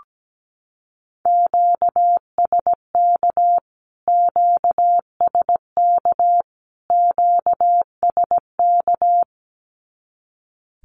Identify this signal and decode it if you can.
Morse code, heard as a keyed tone, about 17 wpm, 700 Hz. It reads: QSK QSK QSK
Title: Q-Codes Repeated 3x 17wpm